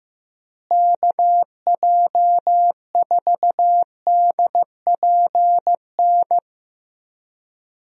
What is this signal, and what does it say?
Morse code, heard as a keyed tone, about 15 wpm, 700 Hz. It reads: KJ4DPN
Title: USA Call Signs - 6 Characters - Repeated 3x - 15wpm